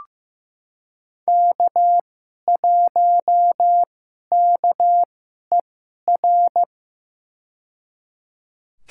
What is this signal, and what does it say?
Morse code, heard as a keyed tone, about 15 wpm, spaced as for 11 wpm, 700 Hz. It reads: K1KER
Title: USA Call Signs - 2x Character Spacing - 15wpm